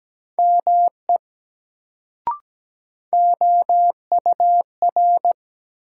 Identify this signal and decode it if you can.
Morse code, heard as a keyed tone, about 17 wpm, 700 Hz. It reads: ME OUR
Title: Sentences from Top 200 Words - New Word Drill - 17wpm